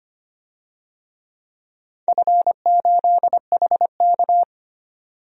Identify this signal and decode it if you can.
Morse code, heard as a keyed tone, about 25 wpm, 700 Hz. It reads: F8HK
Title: DXCC-100 Call Signs - 4 Characters - 25wpm